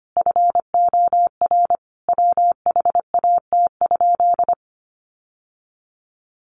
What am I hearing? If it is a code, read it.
Morse code, heard as a keyed tone, about 25 wpm, 700 Hz. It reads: FOR WHAT?
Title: Sentences from Top 700 Words 25wpm